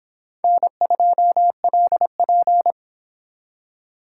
Morse code, heard as a keyed tone, about 28 wpm, 700 Hz. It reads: N2LP